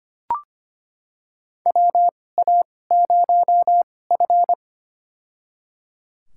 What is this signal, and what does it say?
Morse code, heard as a keyed tone, about 25 wpm, spaced as for 18 wpm, 700 Hz. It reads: WA0F
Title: USA Call Signs - 4 Characters - 2x Character Spacing - 25wpm